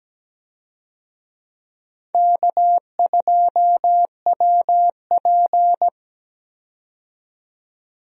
Morse code, heard as a keyed tone, about 17 wpm, 700 Hz. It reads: K2WP
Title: USA Call Signs - 4 Characters - 17wpm